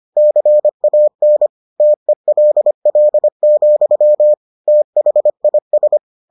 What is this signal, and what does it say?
Morse code, heard as a keyed tone, about 25 wpm, 600 Hz. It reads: CAN TELL, THIS